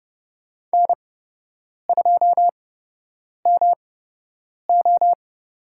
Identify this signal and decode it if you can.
Morse code, heard as a keyed tone, about 30 wpm, spaced as for 8 wpm, 700 Hz. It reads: N2MO
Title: USA Call Signs - 5 Characters - 8x Character Spacing - 30wpm